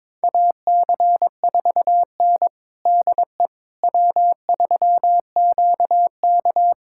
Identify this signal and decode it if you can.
Morse code, heard as a keyed tone, about 21 wpm, 700 Hz. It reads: AC4N DE W3QK
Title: USA Callsign DE Callsign - 4 Characters - 22wpm